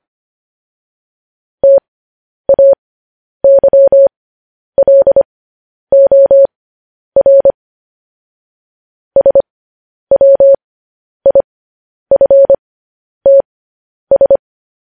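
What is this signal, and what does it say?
Morse code, heard as a keyed tone, about 25 wpm, spaced as for 10 wpm, 550 Hz. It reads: TAYLOR SWIFTS
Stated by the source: Morse code news headlines broadcast, opening with vvv.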